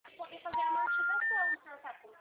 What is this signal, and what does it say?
Special information tone, with speech.
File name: out- -2010- - - .1135